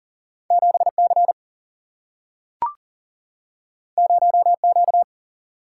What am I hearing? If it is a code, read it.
Morse code, heard as a keyed tone, about 39 wpm, 700 Hz. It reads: 7C 0Q
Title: Single-Letter Single-Number 40wpm